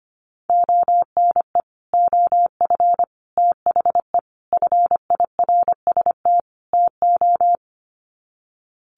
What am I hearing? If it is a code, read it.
Morse code, heard as a keyed tone, about 25 wpm, 690 Hz. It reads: ONE OF THE FIRST TO
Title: Sets of 5 Words Encore 25wpm